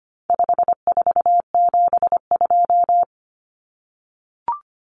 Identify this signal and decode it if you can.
Morse code, heard as a keyed tone, about 25 wpm, 700 Hz. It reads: 5472
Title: Common Cut Numbers - 4-Digit 25wpm